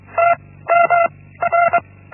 Morse code, heard as a keyed tone, about 24 wpm, 680 Hz. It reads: T M R
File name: Linksmakalnis-1